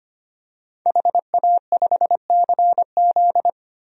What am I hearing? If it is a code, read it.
Morse code, heard as a keyed tone, about 25 wpm, 700 Hz. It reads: HA5CZ
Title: DXCC-100 Call Signs - 5 Characters - 25wpm